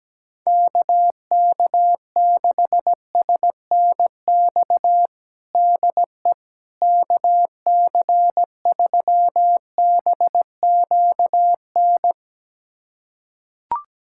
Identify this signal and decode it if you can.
Morse code, heard as a keyed tone, about 17 wpm, 700 Hz. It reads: KK6SNX DE KC3BQN
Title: USA Callsign DE Callsign - 6 Characters - 17wpm